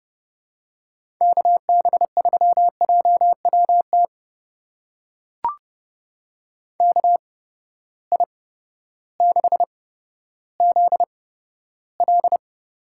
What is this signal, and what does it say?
Morse code, heard as a keyed tone, about 30 wpm, spaced as for 14 wpm, 700 Hz. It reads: KB3JWT KI6ZL
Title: USA Call Signs - 6 Characters - 8x Character Spacing - 30wpm